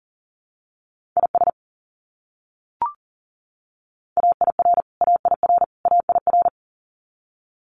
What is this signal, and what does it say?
Morse code, heard as a keyed tone, about 40 wpm, 700 Hz. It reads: IS AIR AIR AIR
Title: Top 300 Words Repeated 40wpm